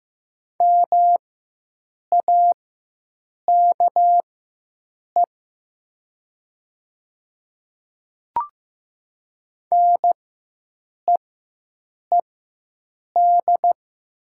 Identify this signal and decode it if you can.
Morse code, heard as a keyed tone, about 15 wpm, spaced as for 7 wpm, 700 Hz. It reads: MAKE NEED
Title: 4x Character Spacing - Top 200 Words - 15wpm